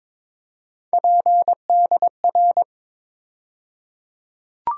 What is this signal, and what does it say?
Morse code, heard as a keyed tone, about 22 wpm, 700 Hz. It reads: PDR
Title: USA Call Sign Postfixes - 3 Letters 22wpm